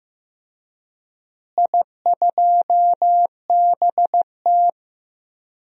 Morse code, heard as a keyed tone, about 15 wpm, 700 Hz. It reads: I2BT
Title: DXCC-100 Call Signs - 4 Characters - Repeated 3x - 15wpm